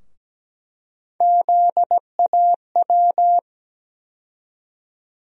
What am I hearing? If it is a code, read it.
Morse code, heard as a keyed tone, about 17 wpm, 700 Hz. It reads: ZAW
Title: USA Call Sign Postfixes - 3 Letters 17wpm